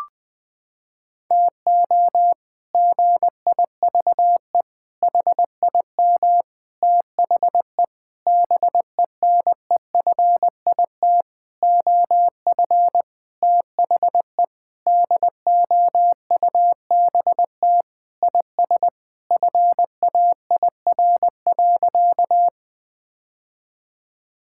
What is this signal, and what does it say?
Morse code, heard as a keyed tone, about 21 wpm, 700 Hz. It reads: TO GIVE HIM THE BENEFIT OF THE DOUBT IS FAIR.
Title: Sentences with 8-Word Phrases - 20wpm